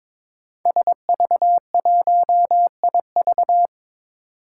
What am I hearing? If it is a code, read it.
Morse code, heard as a keyed tone, about 22 wpm, 700 Hz. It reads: SV1IV
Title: Five Challenging Letter-Number Combinations 22wpm